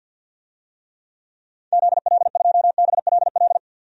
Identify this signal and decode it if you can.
Morse code, heard as a keyed tone, about 53 wpm, 700 Hz. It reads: ZL2BLL
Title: DXCC-100 Call Signs - 6 Characters - 50wpm